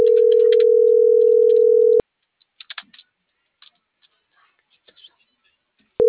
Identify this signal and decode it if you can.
Ringback tone (North America), 440 + 480 Hz.
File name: CENTRAL NMP / out melanie.diaznmp - .41736